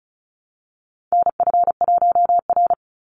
Morse code, heard as a keyed tone, about 34 wpm, 700 Hz. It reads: NF1R
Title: NA SOTA Chaser Callsigns 35wpm